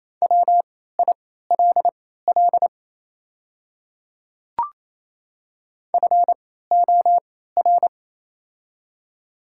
Morse code, heard as a keyed tone, about 28 wpm, spaced as for 16 wpm, 700 Hz. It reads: WILL FOR
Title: 3x Character Spacing - Top 100 Words - 28wpm